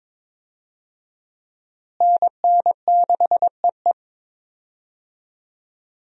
Morse code, heard as a keyed tone, about 23 wpm, 700 Hz. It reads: NN6EE
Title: USA Call Signs - 5 Characters - Repeated 2x - 22wpm